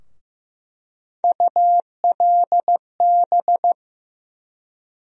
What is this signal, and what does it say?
Morse code, heard as a keyed tone, about 15 wpm, 700 Hz. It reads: ULB
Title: USA Call Sign Postfixes - 3 Letters 15wpm